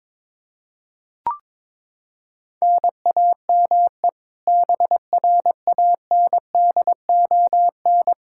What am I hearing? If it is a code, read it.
Morse code, heard as a keyed tone, about 22 wpm, 700 Hz. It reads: NAME BRANDON
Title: K1USN SST QSO Practice 22wpm